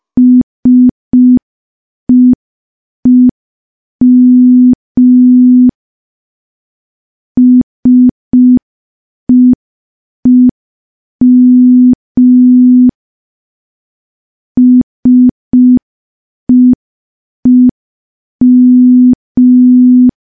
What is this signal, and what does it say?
Morse code, heard as a keyed tone, about 5 wpm, 250 Hz. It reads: SEEM SEEM SEEM
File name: morse / morse-training-05WPM